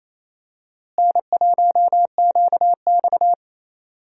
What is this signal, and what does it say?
Morse code, heard as a keyed tone, about 28 wpm, 700 Hz. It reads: N1QX
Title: USA Call Signs - 4 Characters - Repeated 3x - 28wpm